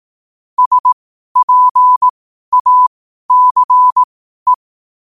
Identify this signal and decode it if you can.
Morse code, heard as a keyed tone, about 18 wpm, spaced as for 12 wpm, 1000 Hz. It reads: SPACE
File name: qotdfiles / 12wpm